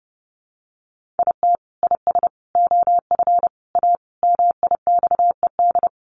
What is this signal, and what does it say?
Morse code, heard as a keyed tone, about 30 wpm, 700 Hz. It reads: IT IS OF A MIXED
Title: Sentences from Top 1500 Words 30wpm